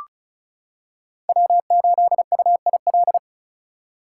Morse code, heard as a keyed tone, about 36 wpm, 700 Hz. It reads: W8UIL
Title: K1USN SST QSO Practice - 35wpm